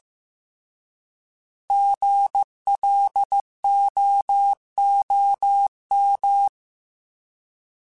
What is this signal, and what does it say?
Morse code, heard as a keyed tone, about 15 wpm, 800 Hz. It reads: GLOOM
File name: Alphabetical / Alphabetical mp3s / Gs @13